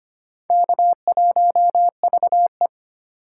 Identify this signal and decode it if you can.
Morse code, heard as a keyed tone, about 25 wpm, 700 Hz. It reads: K1VE
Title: USA Call Signs - 4 Characters - 3x Character Spacing - 25wpm